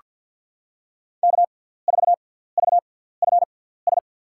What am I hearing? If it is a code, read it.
Morse code, heard as a keyed tone, about 49 wpm, spaced as for 17 wpm, 700 Hz. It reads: K4VFS
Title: USA Call Signs - 5 Characters - 6x Character Spacing - 50wpm